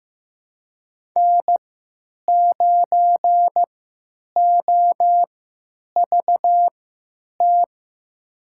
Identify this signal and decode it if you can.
Morse code, heard as a keyed tone, about 15 wpm, spaced as for 9 wpm, 700 Hz. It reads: N9OVT
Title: USA Call Signs - 5 Characters - 3x Character Spacing - 15wpm